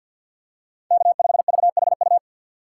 Morse code, heard as a keyed tone, about 52 wpm, 700 Hz. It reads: K5VHU
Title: USA Call Signs - 5 Characters - 6x Character Spacing - 50wpm